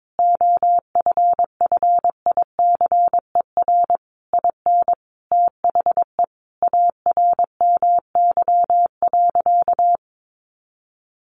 Morse code, heard as a keyed tone, about 22 wpm, 700 Hz. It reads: OFFICER IN THE ARMY.
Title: Sentences from Top 1200 Words 22wpm